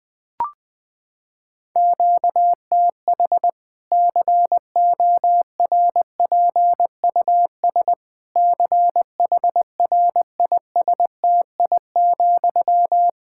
Morse code, heard as a keyed tone, about 20 wpm, 700 Hz. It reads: QTH CORPUS CHRISTI,
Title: K1USN SST QSO Practice 20wpm